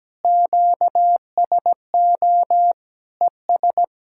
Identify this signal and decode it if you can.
Morse code, heard as a keyed tone, about 17 wpm, 700 Hz. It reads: QSO ES
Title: K1USN SST QSO Practice 17wpm